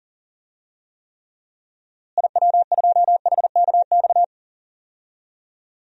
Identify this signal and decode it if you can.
Morse code, heard as a keyed tone, about 37 wpm, 700 Hz. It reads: IW2HKX
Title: DXCC-100 Call Signs - 6 Characters - Repeated 3x - 40wpm